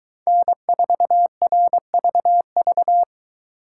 Morse code, heard as a keyed tone, about 22 wpm, 700 Hz. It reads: N4RVV